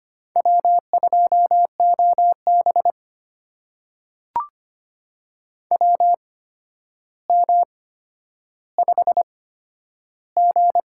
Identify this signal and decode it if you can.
Morse code, heard as a keyed tone, about 25 wpm, spaced as for 11 wpm, 700 Hz. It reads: W2OB WM5G
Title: USA Call Signs - 4 Characters - 8x Character Spacing - 25wpm